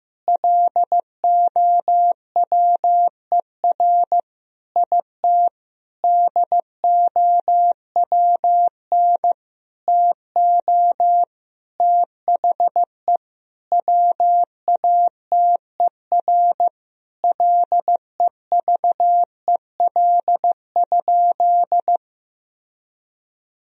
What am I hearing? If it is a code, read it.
Morse code, heard as a keyed tone, about 16 wpm, 700 Hz. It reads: LOWER IT DOWN TO THE WATER LEVEL?
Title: Sentences from Top 1000 Words - Review Part 4 - 15wpm